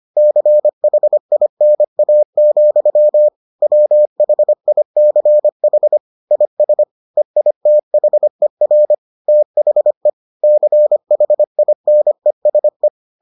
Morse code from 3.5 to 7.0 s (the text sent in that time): WHICH IS